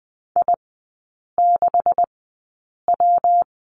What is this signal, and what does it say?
Morse code, heard as a keyed tone, about 20 wpm, spaced as for 8 wpm, 700 Hz. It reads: I6W
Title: 152 Lesson 25e - Letter p - Three Characters 20wpm